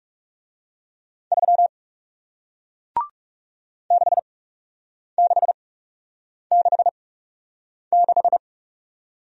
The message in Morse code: 3 6666

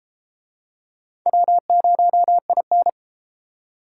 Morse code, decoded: W0IN

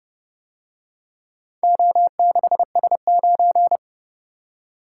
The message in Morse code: O6S9

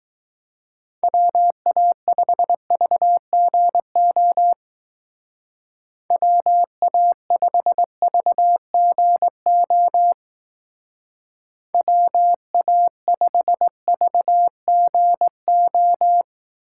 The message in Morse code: WA5VGO WA5VGO WA5VGO

